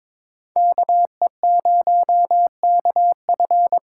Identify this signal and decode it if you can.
Morse code, heard as a keyed tone, about 22 wpm, 700 Hz. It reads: KE0KF
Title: USA Call Signs - 6x Character Spacing - 22wpm